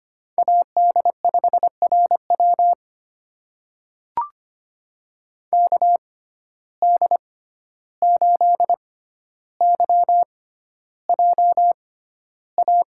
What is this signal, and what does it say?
Morse code, heard as a keyed tone, about 25 wpm, spaced as for 12 wpm, 700 Hz. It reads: AD5RW KD8YJA
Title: USA Call Signs - 6x Character Spacing - 25wpm